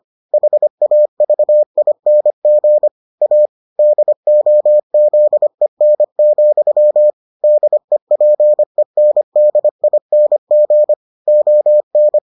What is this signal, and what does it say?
Morse code, heard as a keyed tone, about 25 wpm, 600 Hz. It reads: HAVING A DOZEN, DEPENDING ON